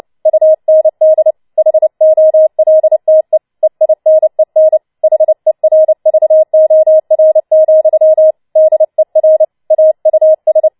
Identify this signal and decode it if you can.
Morse code, heard as a keyed tone, about 29 wpm, 630 Hz. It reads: UND HOLTE EINEN HERVOR, DER AUS
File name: Schneewittchen Teil3 30wpm